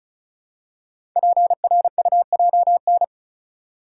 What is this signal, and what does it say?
Morse code, heard as a keyed tone, about 36 wpm, 690 Hz. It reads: PRUJN
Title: Code Groups - 5 Letters - 35wpm